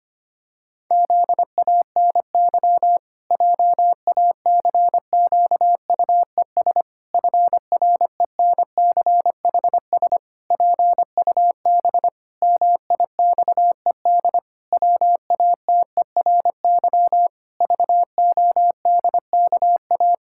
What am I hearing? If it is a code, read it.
Morse code, heard as a keyed tone, about 25 wpm, 700 Hz. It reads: ZANY JACQUES FRENCHS PUB MIXED WATERY VODKA
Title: Pangram Challenge v2 - 25wpm